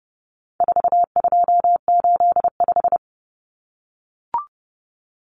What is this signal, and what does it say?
Morse code, heard as a keyed tone, about 30 wpm, 700 Hz. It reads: 4285
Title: Common Cut Numbers - 4-Digit 30wpm